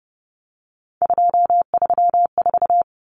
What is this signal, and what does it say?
Morse code, heard as a keyed tone, about 30 wpm, 700 Hz. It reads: 234
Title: Numbers - 3-Digit 30wpm